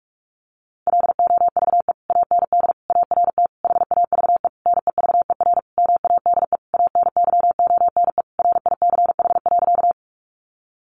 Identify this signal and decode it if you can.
Morse code, heard as a keyed tone, about 44 wpm, 700 Hz. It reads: LOVE AND ART HAVE NEVER MADE ANYONE RICH.
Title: Sentences from Top 1100 Words 45wpm